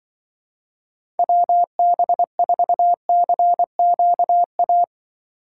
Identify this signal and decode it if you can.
Morse code, heard as a keyed tone, about 24 wpm, 700 Hz. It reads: WB4CQA
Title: Speed Racing - USA 2x3 Call Signs 20wpm